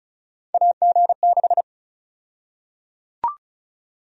Morse code, heard as a keyed tone, about 34 wpm, 700 Hz. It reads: AG6
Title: USA Call Sign Prefixes - 35wpm